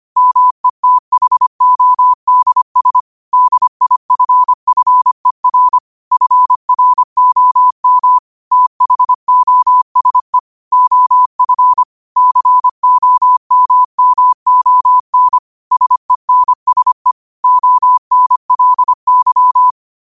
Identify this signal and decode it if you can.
Morse code, heard as a keyed tone, about 25 wpm, 1000 Hz. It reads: METHODS DIFFER FROM THOSE OF COMMON SENSE ONLY